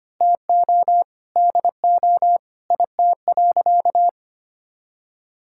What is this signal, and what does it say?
Morse code, heard as a keyed tone, about 25 wpm, 700 Hz. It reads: TO DO IT.
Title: Sentences from Top 1600 Words 25wpm